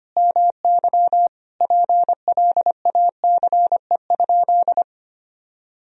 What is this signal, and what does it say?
Morse code, heard as a keyed tone, about 25 wpm, 700 Hz. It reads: MY PLACE?
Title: Sentences from Top 200 Words and Abbreviations 25wpm